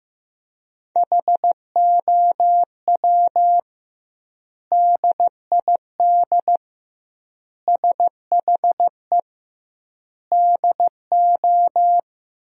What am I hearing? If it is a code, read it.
Morse code, heard as a keyed tone, about 16 wpm, 700 Hz. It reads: HOW DID SHE DO
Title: 2x Word-Spacing - Sentences from Top 100 Words - 15wpm